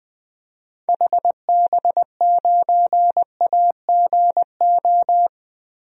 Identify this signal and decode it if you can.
Morse code, heard as a keyed tone, about 20 wpm, 700 Hz. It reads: HB9AGO
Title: DXCC-100 Call Signs - 6 Characters - 20wpm